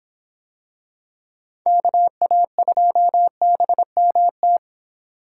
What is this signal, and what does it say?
Morse code, heard as a keyed tone, about 26 wpm, 700 Hz. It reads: KA2BMT